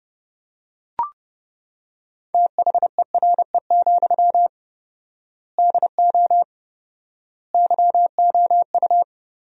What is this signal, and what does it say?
Morse code, heard as a keyed tone, about 30 wpm, 700 Hz. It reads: THERE, DO YOU
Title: 4x Word-Spacing - Sentences from Top 300 Words - 30wpm